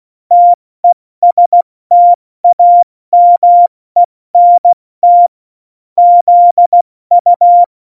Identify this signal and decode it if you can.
Morse code, heard as a keyed tone, about 16 wpm, 700 Hz. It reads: TESTAMENT ZU
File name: Morsekurs / Witze / Witz 18. Februar 2020, 70 F80